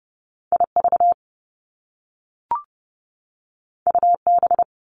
Morse code, heard as a keyed tone, about 30 wpm, 700 Hz. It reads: IV UB
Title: Top Two Letter Combinations in Words 30wpm